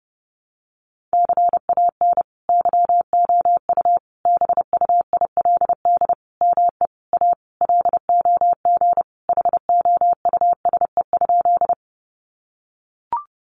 Morse code, heard as a keyed tone, about 30 wpm, 700 Hz. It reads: CAN YOU BUILD ME A LOG HOUSE?
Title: Sentences from Top 1400 Words 30wpm